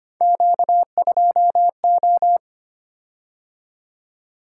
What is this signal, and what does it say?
Morse code, heard as a keyed tone, about 26 wpm, 700 Hz. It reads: Q2O
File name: Three Letter-Number Combinations - 25wpm